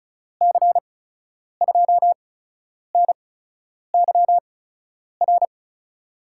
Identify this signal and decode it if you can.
Morse code, heard as a keyed tone, about 34 wpm, spaced as for 10 wpm, 700 Hz. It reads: C2NYR
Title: USA Call Signs - 6 Characters - 8x Character Spacing - 35wpm